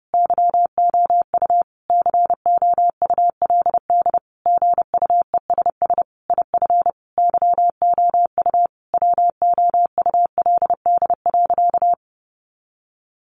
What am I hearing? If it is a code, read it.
Morse code, heard as a keyed tone, about 30 wpm, 700 Hz. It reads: YOU COULD GUESS IF YOU WOULD.
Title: Sentences from Top 1500 Words 30wpm